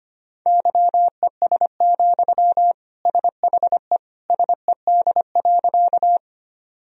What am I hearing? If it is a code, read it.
Morse code, heard as a keyed tone, about 25 wpm, 700 Hz. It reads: YES, SHE SED.
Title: Sentences from Top 500 Words and Abbreviations 25wpm